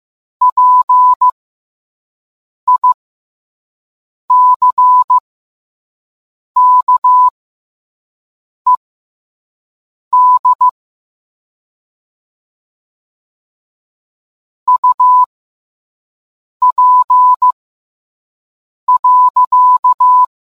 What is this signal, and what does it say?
Morse code, heard as a keyed tone, about 15 wpm, spaced as for 5 wpm, 1000 Hz. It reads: PICKED UP.